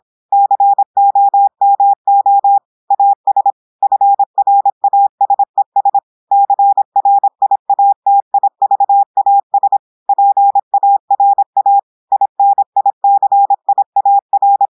Morse code, heard as a keyed tone, about 26 wpm, 800 Hz. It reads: COMO AS FRASES CRIATIVAS PARA INICIAR
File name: portugais texte en clair 2 vitesse 26 mots minutes